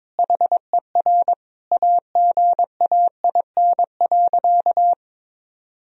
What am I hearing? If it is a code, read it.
Morse code, heard as a keyed tone, about 22 wpm, 700 Hz. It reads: HER AGAIN.
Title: Sentences from Top 1000 Words - Review Part 3 - 22wpm